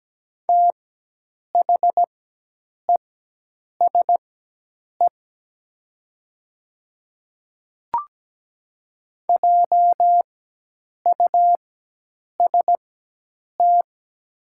Morse code, heard as a keyed tone, about 17 wpm, spaced as for 8 wpm, 700 Hz. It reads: THESE JUST
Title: 4x Character Spacing - Top 300 Words - 17wpm